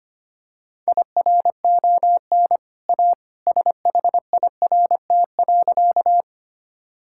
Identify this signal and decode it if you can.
Morse code, heard as a keyed tone, about 25 wpm, 700 Hz. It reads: IRON A SHIRT.
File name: Easy Sentences with Hints - 25wpm